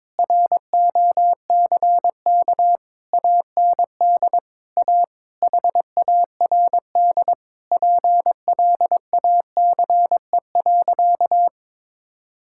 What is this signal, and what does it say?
Morse code, heard as a keyed tone, about 22 wpm, 700 Hz. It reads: ROCK AND A HARD PLACE.
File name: practice / 22wpm / Sentences with 8-Word Phrases - 22wpm